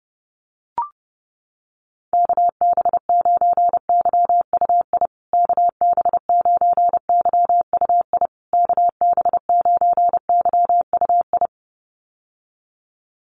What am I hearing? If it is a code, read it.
Morse code, heard as a keyed tone, about 30 wpm, 700 Hz. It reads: KB9YUI KB9YUI KB9YUI
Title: USA Call Signs Repeated 3x 30wpm